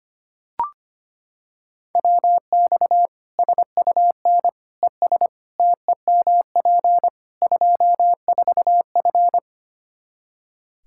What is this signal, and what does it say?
Morse code, heard as a keyed tone, about 25 wpm, 700 Hz. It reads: WX SUN ES TEMP 24F
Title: Practice QSO - Exchange 2 - Info - 25wpm